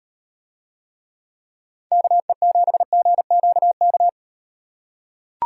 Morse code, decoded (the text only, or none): KE7GQK